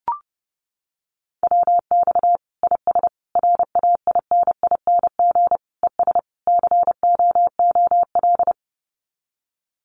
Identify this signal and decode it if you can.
Morse code, heard as a keyed tone, about 30 wpm, 700 Hz. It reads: WX IS RAINING ES COOL
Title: Common QSO Phrases 30wpm